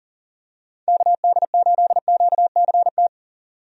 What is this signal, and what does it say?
Morse code, heard as a keyed tone, about 40 wpm, 700 Hz. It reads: KD8QCT
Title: USA Call Signs - 6 Characters - 40wpm